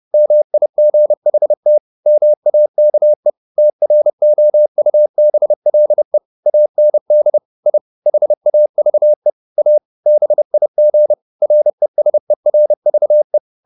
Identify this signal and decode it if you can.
Morse code, heard as a keyed tone, about 30 wpm, 600 Hz. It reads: MIGHT MAKE TROUBLE AND I HAVE A BIG RESERVE
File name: the old man and the sea / the old man and the sea 1 30wpm